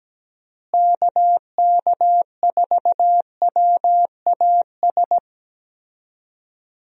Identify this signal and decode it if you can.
Morse code, heard as a keyed tone, about 17 wpm, 700 Hz. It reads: KK4WAS
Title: USA Call Signs - 6 Characters - Repeated 3x - 17wpm